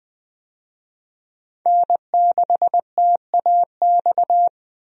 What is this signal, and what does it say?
Morse code, heard as a keyed tone, about 20 wpm, 700 Hz. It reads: N6TAX
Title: USA Call Signs - 5 Characters - Repeated 3x - 20wpm